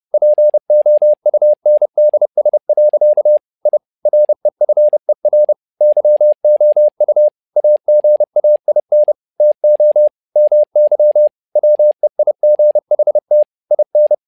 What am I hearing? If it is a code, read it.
Morse code, heard as a keyed tone, about 30 wpm, 600 Hz. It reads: POUNDS. I REFER YOU AGAIN TO MY WEIGHT IN
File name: learning center / tower trouble / tower trouble 30wpm